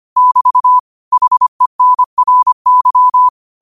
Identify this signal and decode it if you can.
Morse code, heard as a keyed tone, about 25 wpm, 1000 Hz. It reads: <BT> HENRY